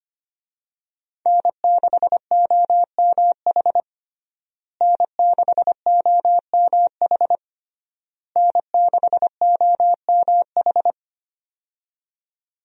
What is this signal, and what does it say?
Morse code, heard as a keyed tone, about 25 wpm, 700 Hz. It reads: N6OMH N6OMH N6OMH
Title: USA Call Signs - 5 Characters - Repeated 3x - 25wpm